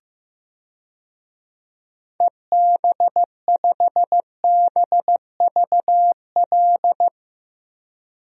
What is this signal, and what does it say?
Morse code, heard as a keyed tone, about 16 wpm, 700 Hz. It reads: EB5BVL
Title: DXCC-100 Call Signs - 6 Characters - Repeated 2x - 15wpm